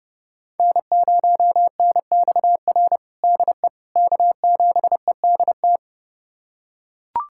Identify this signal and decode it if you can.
Morse code, heard as a keyed tone, about 30 wpm, 700 Hz. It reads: N0NXR DE K7EDT
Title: USA Callsign DE Callsign - 5 Characters - 30wpm